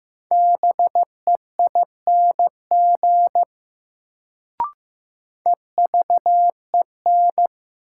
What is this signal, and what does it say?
Morse code, heard as a keyed tone, about 15 wpm, 700 Hz. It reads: BEING EVEN
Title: Sets of 4 Words - New Word Drill - 15wpm